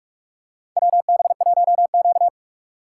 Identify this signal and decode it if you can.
Morse code, heard as a keyed tone, about 44 wpm, 700 Hz. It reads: WB1Q
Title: USA Call Signs - 3x Character Spacing - 45wpm